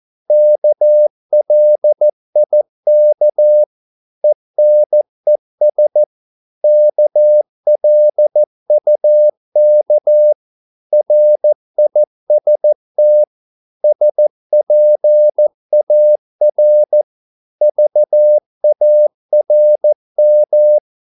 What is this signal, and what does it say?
Morse code, heard as a keyed tone, about 14 wpm, 600 Hz. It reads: KLIK ENES KLUK RIST SPAR VARM
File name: Korte ord DK 14wpm